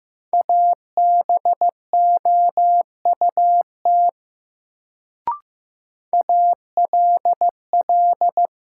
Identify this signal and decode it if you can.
Morse code, heard as a keyed tone, about 15 wpm, 700 Hz. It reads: ABOUT ALL
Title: Sentences from Top 100 Words - New Word Drill - 15wpm